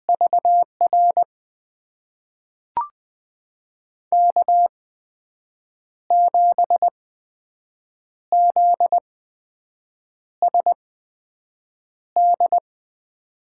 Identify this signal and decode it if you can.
Morse code, heard as a keyed tone, about 20 wpm, spaced as for 6 wpm, 700 Hz. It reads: VR K7ZSD
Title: USA Call Signs - 8x Character Spacing - 20wpm